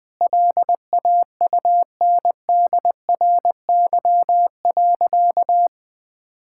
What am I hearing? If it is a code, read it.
Morse code, heard as a keyed tone, about 20 wpm, 700 Hz. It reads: LAUNDRY.
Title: Easy Sentences with Hints - 20wpm